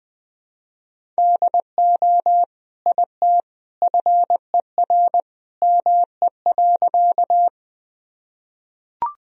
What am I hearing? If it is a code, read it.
Morse code, heard as a keyed tone, about 21 wpm, 700 Hz. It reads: DO IT FER ME.
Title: Sentences from Top 200 Words and Abbreviations 20wpm